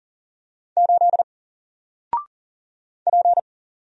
Morse code, heard as a keyed tone, about 40 wpm, 700 Hz. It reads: 8 P
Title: Single Letter-Number Alt 40wpm